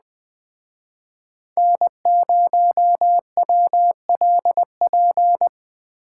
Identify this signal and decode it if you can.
Morse code, heard as a keyed tone, about 20 wpm, 700 Hz. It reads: N0WLP
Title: Speed Racing - USA 1x3 Call Signs 17wpm